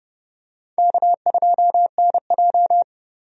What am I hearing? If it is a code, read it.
Morse code, heard as a keyed tone, about 30 wpm, 700 Hz. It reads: K2NJ
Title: USA Call Signs - 4 Characters - Repeated 2x - 30wpm